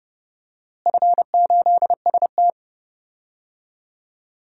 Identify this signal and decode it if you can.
Morse code, heard as a keyed tone, about 30 wpm, 700 Hz. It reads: F8ST
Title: DXCC-100 Call Signs - 4 Characters - Repeated 3x - 30wpm